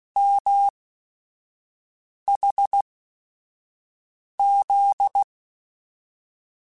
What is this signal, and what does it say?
Morse code, heard as a keyed tone, about 16 wpm, spaced as for 5 wpm, 800 Hz. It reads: MHZ